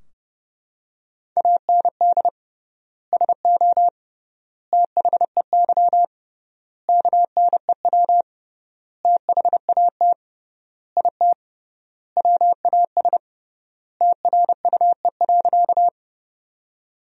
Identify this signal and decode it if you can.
Morse code, heard as a keyed tone, about 30 wpm, 700 Hz. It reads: AND SO THEY KNEW THAT IT WAS TRUE.
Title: 3x Word-Spacing - Sentences from Top 400 Words - 30wpm